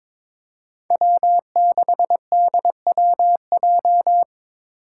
Morse code, heard as a keyed tone, about 21 wpm, 700 Hz. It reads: W6DWJ
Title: Five Challenging Letter-Number Combinations 22wpm